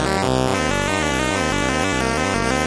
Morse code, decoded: none